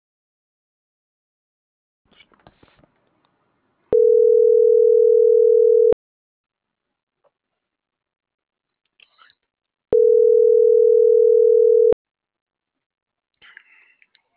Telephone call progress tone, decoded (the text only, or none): ringback tone